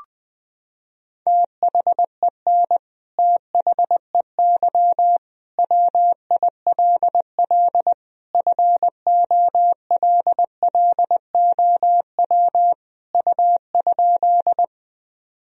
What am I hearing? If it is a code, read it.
Morse code, heard as a keyed tone, about 21 wpm, 700 Hz. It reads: THEN THEY WILL FOLLOW U?
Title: Sentences from Top 200 Words and Abbreviations 20wpm